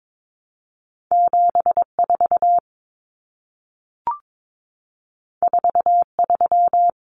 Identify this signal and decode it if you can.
Morse code, heard as a keyed tone, about 22 wpm, 700 Hz. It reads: 74 43
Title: Common Cut Numbers - 2-Digit 22wpm